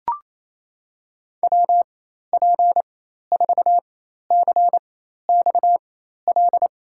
Morse code, heard as a keyed tone, about 28 wpm, spaced as for 13 wpm, 700 Hz. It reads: WP4CXL
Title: USA Call Signs - 6 Characters - 4x Character Spacing - 28wpm